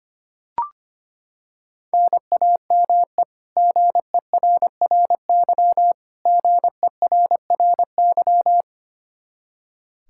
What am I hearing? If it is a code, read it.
Morse code, heard as a keyed tone, about 25 wpm, 700 Hz. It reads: NAME GERRY GERRY
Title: Practice QSO - Exchange 1 - Beginning - 25wpm